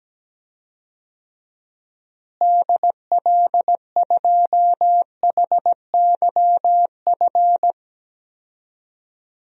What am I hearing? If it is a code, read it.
Morse code, heard as a keyed tone, about 17 wpm, 700 Hz. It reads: DL2HYF
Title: DXCC-100 Call Signs - 6 Characters - Repeated 2x - 17wpm